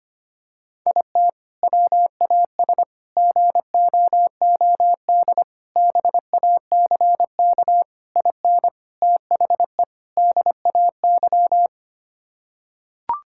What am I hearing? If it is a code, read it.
Morse code, heard as a keyed tone, about 25 wpm, 700 Hz. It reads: IT WAS GOOD BACK IN THE DAY
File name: practice / 25wpm / Sentences with 3-Word Phrases - 25wpm